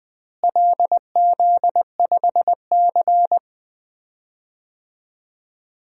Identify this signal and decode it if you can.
Morse code, heard as a keyed tone, about 20 wpm, 700 Hz. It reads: LZ5C
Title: DXCC-100 Call Signs - 4 Characters - Repeated 3x - 20wpm